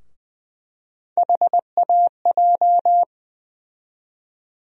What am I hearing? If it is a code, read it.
Morse code, heard as a keyed tone, about 20 wpm, 700 Hz. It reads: HAJ
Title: USA Call Sign Postfixes - 3 Letters 20wpm